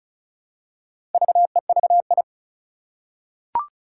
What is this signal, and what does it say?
Morse code, heard as a keyed tone, about 36 wpm, 700 Hz. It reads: VEVI
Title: Four Challenging Letter-Number Combinations 35wpm